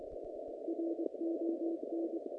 Morse code, heard as a keyed tone, about 23 wpm, 330 Hz, with noise. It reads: RON
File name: contests / naqp-cw-k5tr-op-k5pi- -0100